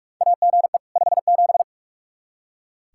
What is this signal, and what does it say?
Morse code, heard as a keyed tone, about 44 wpm, 700 Hz. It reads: AGE 57
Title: Practice QSO - Exchange 3 - More Info - 45wpm